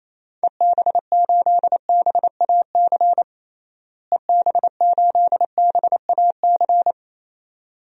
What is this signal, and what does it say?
Morse code, heard as a keyed tone, about 28 wpm, 700 Hz. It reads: EB8BAC EB8BAC
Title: DXCC-100 Call Signs - 6 Characters - Repeated 2x - 28wpm